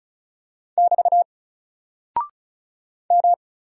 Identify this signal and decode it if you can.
Morse code, heard as a keyed tone, about 36 wpm, 700 Hz. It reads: <BT> M
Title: Single Character (Letter:Number:Punctuation:Prosign) Alt 35wpm